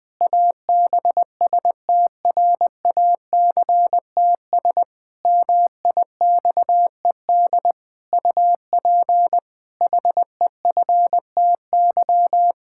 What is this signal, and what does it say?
Morse code, heard as a keyed tone, about 21 wpm, 700 Hz. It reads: ABSTRACTS MIXED UP HEFTY
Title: Pangram Challenge v2 - 20wpm